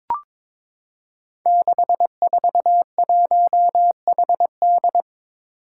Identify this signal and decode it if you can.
Morse code, heard as a keyed tone, about 23 wpm, 700 Hz. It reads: 641HD
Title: Five Challenging Letter-Number Combinations 22wpm